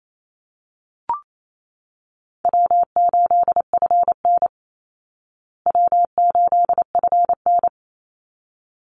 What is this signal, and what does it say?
Morse code, heard as a keyed tone, about 28 wpm, 700 Hz. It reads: W8FN W8FN
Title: Easy Call Signs Repeated 2x - 28wpm